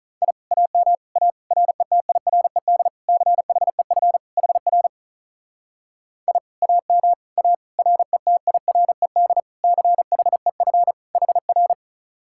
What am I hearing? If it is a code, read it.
Morse code, heard as a keyed tone, about 44 wpm, 700 Hz. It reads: I AM A RETIRED CHEF HR I AM A RETIRED CHEF HR
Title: Speed-Racing Common QSO Phrases Encore - 35wpm